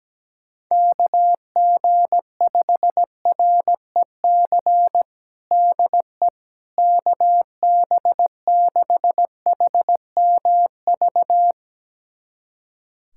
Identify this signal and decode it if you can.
Morse code, heard as a keyed tone, about 17 wpm, 700 Hz. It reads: KG5REC DE KB6HMV
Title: USA Callsign DE Callsign - 6 Characters - 17wpm